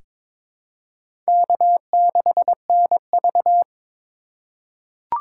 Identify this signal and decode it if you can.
Morse code, heard as a keyed tone, about 22 wpm, 700 Hz. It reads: K6NV
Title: USA Call Signs - 4 Characters - 3x Character Spacing - 22wpm